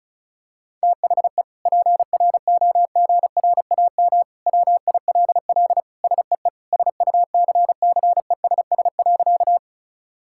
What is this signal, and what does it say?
Morse code, heard as a keyed tone, about 36 wpm, 700 Hz. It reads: THE PROGRAM WILL SEE SUCCESS.
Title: Sentences from Top 1000 Words - Review Part 5 - 35wpm